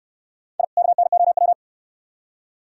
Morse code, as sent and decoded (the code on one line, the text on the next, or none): .. / -.- -. --- .--
I KNOW